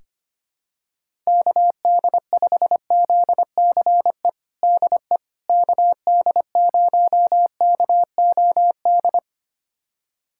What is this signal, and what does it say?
Morse code, heard as a keyed tone, about 25 wpm, 700 Hz. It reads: KD5ZCE DE KD0KOD
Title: USA Callsign DE Callsign - 6 Characters - 25wpm